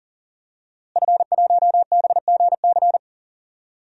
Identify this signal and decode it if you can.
Morse code, heard as a keyed tone, about 40 wpm, 700 Hz. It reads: F1BGC
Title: DXCC-100 Call Signs - 5 Characters - Repeated 3x - 40wpm